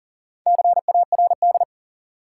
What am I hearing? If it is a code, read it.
Morse code, heard as a keyed tone, about 38 wpm, 700 Hz. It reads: CARD
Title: Sentences from Top 600 Words - New Word Drill - 40wpm